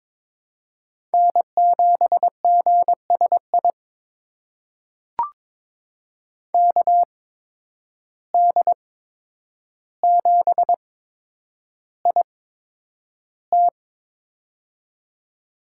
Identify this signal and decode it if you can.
Morse code, heard as a keyed tone, about 22 wpm, spaced as for 9 wpm, 700 Hz. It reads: N7GSI KD7IT
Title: USA Call Signs - 5 Characters - 8x Character Spacing - 22wpm